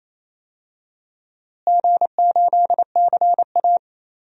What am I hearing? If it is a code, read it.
Morse code, heard as a keyed tone, about 28 wpm, 700 Hz. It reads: G8CA
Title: DXCC-100 Call Signs - 4 Characters - Repeated 2x - 28wpm